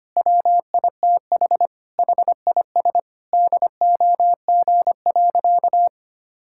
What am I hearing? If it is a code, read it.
Morse code, heard as a keyed tone, about 25 wpm, 700 Hz. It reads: WITH HIS DOG.
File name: practice / 25wpm / Sentences with 2-Word Phrases - 25wpm